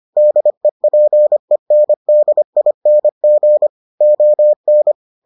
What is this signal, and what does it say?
Morse code, heard as a keyed tone, about 25 wpm, 600 Hz. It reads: DEPENDING ON